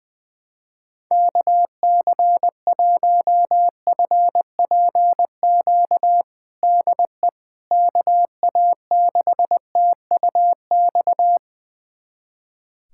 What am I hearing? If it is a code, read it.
Morse code, heard as a keyed tone, about 20 wpm, 700 Hz. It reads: KC1FPQ DE KA6TUX
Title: USA Callsign DE Callsign 20wpm